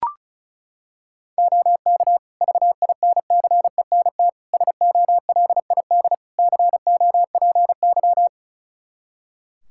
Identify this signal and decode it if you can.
Morse code, heard as a keyed tone, about 36 wpm, 700 Hz. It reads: OK VINCENT SOLID COPY
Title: Practice QSO - Exchange 3 - More Info - 35wpm